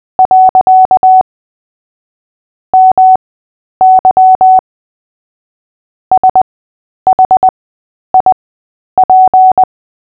Morse code, heard as a keyed tone, about 20 wpm, spaced as for 10 wpm, 750 Hz. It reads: . MY SHIP
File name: cw / fists / Short-Story-3-10wpm